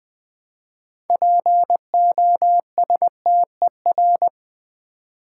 Morse code, heard as a keyed tone, about 20 wpm, 700 Hz. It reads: POSTER
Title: Build A Word - 3 Letters At A Time - 6-Letter Words - 20wpm